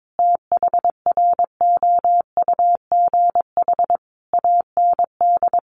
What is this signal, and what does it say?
Morse code, heard as a keyed tone, about 22 wpm, 700 Hz. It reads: THROUGH AND
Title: Sentences from Top 1400 Words 22wpm